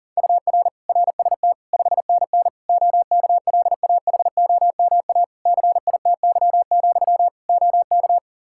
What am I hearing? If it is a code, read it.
Morse code, heard as a keyed tone, about 40 wpm, 700 Hz. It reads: UR RST 5NN OKLAHOMA CITY, OK
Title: Practice QSO - Exchange 1 - Beginning - 40wpm